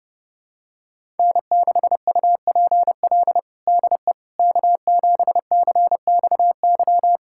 Morse code, heard as a keyed tone, about 30 wpm, 700 Hz. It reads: N6UPL DE K7CXY
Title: USA Callsign DE Callsign - 5 Characters - 30wpm